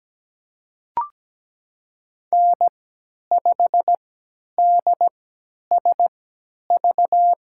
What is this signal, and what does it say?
Morse code, heard as a keyed tone, about 17 wpm, spaced as for 10 wpm, 700 Hz. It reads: N5DSV
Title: USA Call Signs - 3x Character Spacing - 17wpm